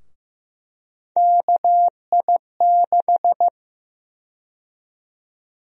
Morse code, heard as a keyed tone, about 15 wpm, 700 Hz. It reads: KI6
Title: USA Call Sign Prefixes - Repeated 2x - 15wpm